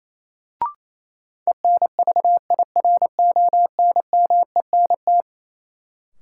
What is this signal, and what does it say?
Morse code, heard as a keyed tone, about 28 wpm, 700 Hz. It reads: ENVIRONMENT
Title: Sentences from Top 1000 Words - Review Part 5 - New Word Drill - 28wpm